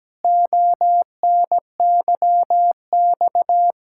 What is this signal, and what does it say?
Morse code, heard as a keyed tone, about 17 wpm, 700 Hz. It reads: ONYX
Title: Pangram Challenge v2 - 17wpm